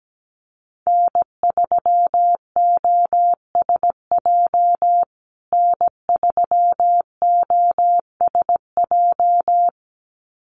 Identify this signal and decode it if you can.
Morse code, heard as a keyed tone, about 17 wpm, 700 Hz. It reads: N3OSJ N3OSJ
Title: USA Call Signs Repeated 2x 17wpm